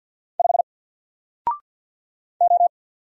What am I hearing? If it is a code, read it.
Morse code, heard as a keyed tone, about 50 wpm, 700 Hz. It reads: 5 O
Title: Single Letter-Number 50wpm